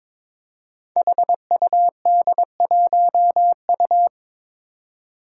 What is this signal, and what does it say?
Morse code, heard as a keyed tone, about 22 wpm, 700 Hz. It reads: HUD1U
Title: Five Challenging Letter-Number Combinations 22wpm